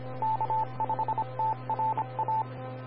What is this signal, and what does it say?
Morse code, heard as a keyed tone, about 26 wpm, 880 Hz. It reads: K5TRA
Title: Austin, TX Amateur Repeaters